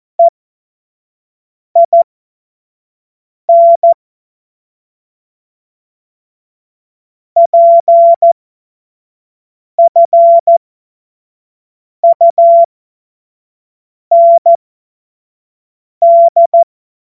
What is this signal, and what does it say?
Morse code, heard as a keyed tone, about 14 wpm, spaced as for 5 wpm, 680 Hz. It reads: EIN PFUND
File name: Morsekurs/Witze/Witz 09 Januar 2024 25 F70